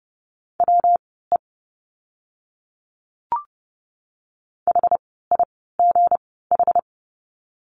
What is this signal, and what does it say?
Morse code, heard as a keyed tone, about 29 wpm, spaced as for 17 wpm, 700 Hz. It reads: WE HIGH
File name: course / 30:10wpm / 237 Lesson 38f - Letter q - Words 30wpm